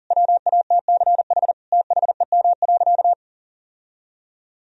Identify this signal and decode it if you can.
Morse code, heard as a keyed tone, about 40 wpm, 700 Hz. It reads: WATCH THEM.
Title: Sentences from Top 1000 Words - Review Part 1 - 40wpm